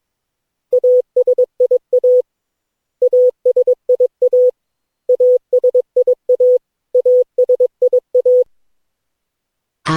Morse code, heard as a keyed tone, about 22 wpm, 500 Hz. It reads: ASIA ASIA ASIA ASIA
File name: cw / Session9 cw training